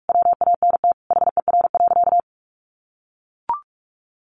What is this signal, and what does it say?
Morse code, heard as a keyed tone, about 44 wpm, 700 Hz. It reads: WANT HER.
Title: Sentences from Top 800 Words 45wpm